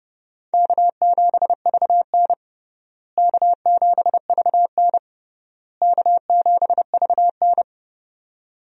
Morse code, heard as a keyed tone, about 30 wpm, 700 Hz. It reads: K7VN K7VN K7VN
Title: USA Call Signs - 4 Characters - Repeated 3x - 30wpm